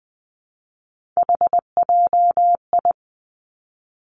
Morse code, heard as a keyed tone, about 20 wpm, 700 Hz. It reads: HJI
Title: Three Challenging Letter-Number Combinations 20wpm